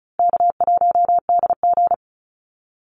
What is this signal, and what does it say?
Morse code, heard as a keyed tone, about 34 wpm, 700 Hz. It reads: K1DG
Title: Call Signs Repeated 2x - 35wpm